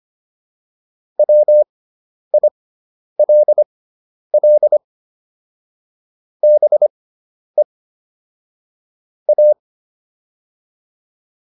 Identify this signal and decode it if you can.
Morse code, heard as a keyed tone, about 25 wpm, spaced as for 10 wpm, 600 Hz. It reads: WILL BE A